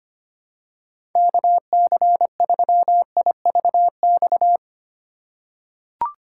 Morse code, heard as a keyed tone, about 25 wpm, 700 Hz. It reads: KC3IVX
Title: USA Call Signs - 8x Character Spacing - 25wpm